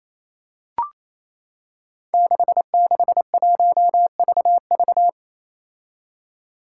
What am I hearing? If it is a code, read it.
Morse code, heard as a keyed tone, about 28 wpm, 700 Hz. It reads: 661VV
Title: Five Challenging Letter-Number Combinations 28wpm